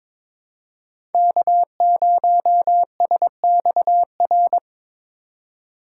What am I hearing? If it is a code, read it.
Morse code, heard as a keyed tone, about 22 wpm, 700 Hz. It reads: K0SXR
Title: USA Call Signs - 5 Characters - 2x Character Spacing - 22wpm